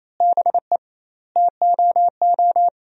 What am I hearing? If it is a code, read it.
Morse code, heard as a keyed tone, about 28 wpm, 700 Hz. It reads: BE TOO
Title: 2x Word-Spacing - Sentences from Top 300 Words - 28wpm